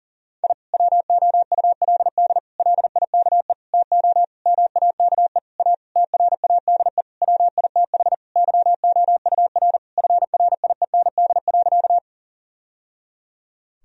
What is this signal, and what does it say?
Morse code, heard as a keyed tone, about 40 wpm, 700 Hz. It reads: I WOULD LIKE TO MAKE A TRADE WITH YOUR FRIEND.
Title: Sentences from Top 1000 Words - Review Part 4 - 40wpm